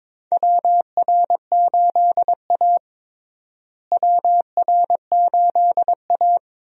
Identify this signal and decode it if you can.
Morse code, heard as a keyed tone, about 22 wpm, 700 Hz. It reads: WR8A WR8A
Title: USA Call Signs - 4 Characters - Repeated 2x - 22wpm